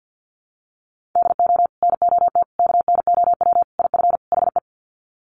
Morse code, heard as a keyed tone, about 50 wpm, 700 Hz. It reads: DO NOT KNOW IF HE
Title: Sets of 5 Words Encore2 50wpm